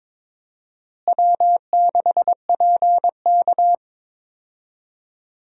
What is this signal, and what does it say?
Morse code, heard as a keyed tone, about 23 wpm, 700 Hz. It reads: W6PK
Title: USA Call Signs - 4 Characters - Repeated 2x - 22wpm